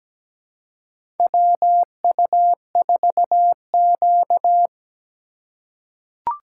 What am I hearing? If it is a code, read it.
Morse code, heard as a keyed tone, about 17 wpm, 700 Hz. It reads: WU4Q
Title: USA Call Signs - 4 Characters - 3x Character Spacing - 17wpm